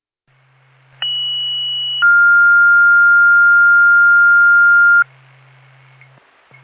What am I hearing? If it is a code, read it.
Two-tone page, heard as a one-second tone, then a three-second tone, about 2950 then 1400 Hz. It is heard from 1.0 to 5.0 s.